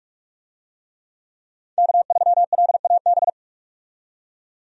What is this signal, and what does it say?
Morse code, heard as a keyed tone, about 46 wpm, 700 Hz. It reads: K3LAB
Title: USA Call Signs - 5 Characters - 45wpm